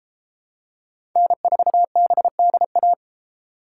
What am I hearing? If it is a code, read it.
Morse code, heard as a keyed tone, about 32 wpm, 700 Hz. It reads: N4BDA